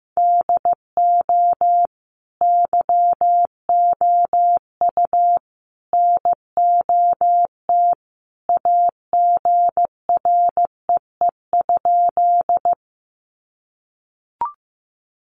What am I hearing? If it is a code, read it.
Morse code, heard as a keyed tone, about 15 wpm, 700 Hz. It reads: DO YOU NOT AGREE?
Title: Sentences from Top 1100 Words 15wpm